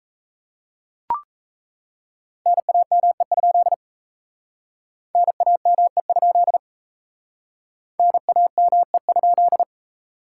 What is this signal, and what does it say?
Morse code, heard as a keyed tone, about 37 wpm, 700 Hz. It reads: NAME? NAME? NAME?